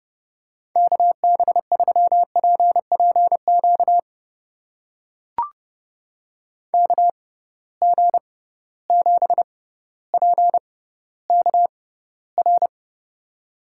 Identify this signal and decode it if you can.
Morse code, heard as a keyed tone, about 30 wpm, spaced as for 15 wpm, 700 Hz. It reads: KB3PPQ KG7PKR
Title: USA Call Signs - 6 Characters - 6x Character Spacing - 30wpm